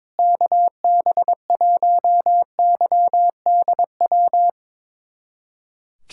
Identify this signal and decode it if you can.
Morse code, heard as a keyed tone, about 22 wpm, 700 Hz. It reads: KB1YDW
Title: USA Callsign DE Callsign - 6 Characters - 22wpm